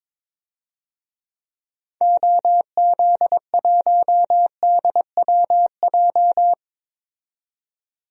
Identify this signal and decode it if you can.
Morse code, heard as a keyed tone, about 22 wpm, 700 Hz. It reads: OZ1DWJ
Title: DXCC-100 Call Signs - 6 Characters - 22wpm